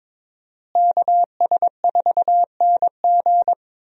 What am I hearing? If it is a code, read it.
Morse code, heard as a keyed tone, about 22 wpm, 700 Hz. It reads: KS4NG
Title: USA Call Signs - 5 Characters - Repeated 2x - 22wpm